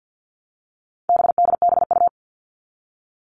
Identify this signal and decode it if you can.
Morse code, heard as a keyed tone, about 50 wpm, 700 Hz. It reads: BDBU
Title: All Cut Numbers - 4-Digit 50wpm